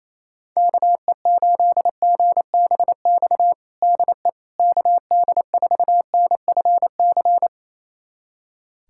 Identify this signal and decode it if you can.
Morse code, heard as a keyed tone, about 27 wpm, 700 Hz. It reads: KE8GBX DE KD4NFC
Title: USA Callsign DE Callsign - 6 Characters - 28wpm